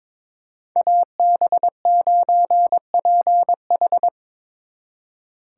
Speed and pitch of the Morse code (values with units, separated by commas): 29 wpm, 700 Hz